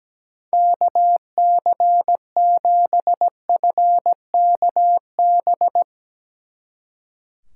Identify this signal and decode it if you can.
Morse code, heard as a keyed tone, about 19 wpm, 700 Hz. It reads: KC7FKB
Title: Speed Racing - USA Call Signs 17wpm